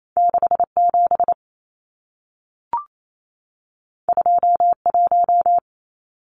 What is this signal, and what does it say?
Morse code, heard as a keyed tone, about 28 wpm, 700 Hz. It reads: 67 21
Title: Numbers - 2-Digit 28wpm